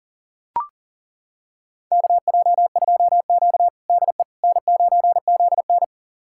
Morse code, heard as a keyed tone, about 40 wpm, 700 Hz. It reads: KJ2Q DE N9ZN
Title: USA Callsign DE Callsign - 4 Characters - 40wpm